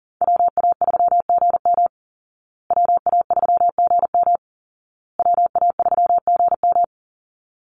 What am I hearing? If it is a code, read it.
Morse code, heard as a keyed tone, about 40 wpm, 700 Hz. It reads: WA3GM WA3GM WA3GM
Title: Call Signs Repeated 2x - 40wpm